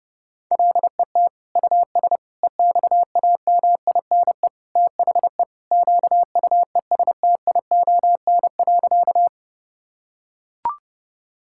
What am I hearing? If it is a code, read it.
Morse code, heard as a keyed tone, about 30 wpm, 700 Hz. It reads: LET US EXAMINE THE QUESTION.
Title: Sentences from Top 1600 Words 30wpm